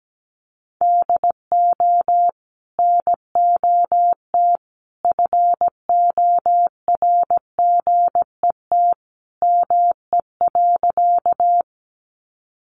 Morse code, heard as a keyed tone, about 17 wpm, 700 Hz. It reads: DO NOT FORGET ME.
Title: Sentences from Top 1200 Words 17wpm